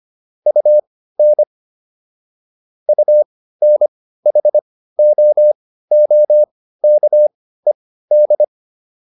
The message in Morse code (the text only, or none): UN UNHOOKED